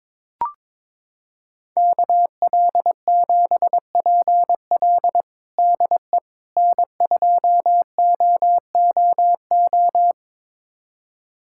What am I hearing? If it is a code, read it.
Morse code, heard as a keyed tone, about 22 wpm, 700 Hz. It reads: KL7PL DE N2OOO
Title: USA Callsign DE Callsign - 5 Characters - 22wpm